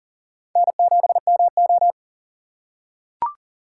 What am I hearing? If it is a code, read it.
Morse code, heard as a keyed tone, about 40 wpm, 700 Hz. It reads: N7MO
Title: USA Call Signs - 4 Characters - 2x Character Spacing - 40wpm